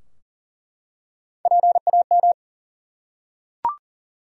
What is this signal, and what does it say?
Morse code, heard as a keyed tone, about 40 wpm, 700 Hz. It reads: PAM
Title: USA Common Names - Max 3 Letters - 40wpm